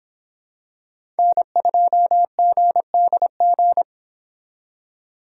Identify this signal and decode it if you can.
Morse code, heard as a keyed tone, about 26 wpm, 700 Hz. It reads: N2GDG